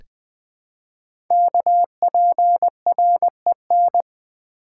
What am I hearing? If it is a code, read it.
Morse code, heard as a keyed tone, about 20 wpm, 700 Hz. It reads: KPREN
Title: Code Groups - 5 Letters - 20wpm